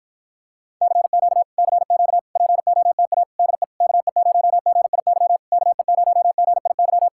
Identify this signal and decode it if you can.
Morse code, heard as a keyed tone, about 53 wpm, 700 Hz. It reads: CQ CQ POTA DE KE0GIY KE0GIY